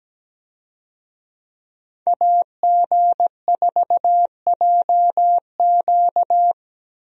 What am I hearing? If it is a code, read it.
Morse code, heard as a keyed tone, about 17 wpm, 700 Hz. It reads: AG4JQ
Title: USA Call Signs - 5 Characters - Repeated 2x - 17wpm